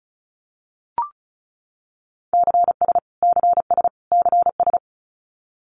Morse code, heard as a keyed tone, about 34 wpm, 700 Hz. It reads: CS CS CS
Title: Abbreviations Repeated 3x - 35wpm